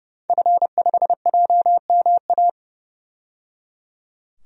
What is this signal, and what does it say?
Morse code, heard as a keyed tone, about 29 wpm, 700 Hz. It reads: F5JMA
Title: DXCC-100 Call Signs - 5 Characters - 30wpm